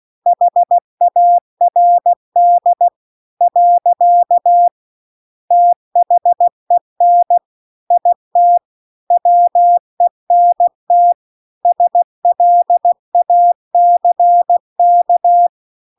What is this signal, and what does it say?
Morse code, heard as a keyed tone, about 16 wpm, 700 Hz. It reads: HARD . THEN IT WENT SLACK